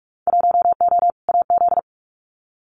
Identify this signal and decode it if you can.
Morse code, heard as a keyed tone, about 46 wpm, 700 Hz. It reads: 1O AZ
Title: Winter Field Day Exchanges 45wpm